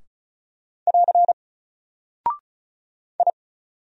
Morse code, heard as a keyed tone, about 36 wpm, 700 Hz. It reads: <AR> I
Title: Single Character (Letter:Number:Punctuation:Prosign) Alt 35wpm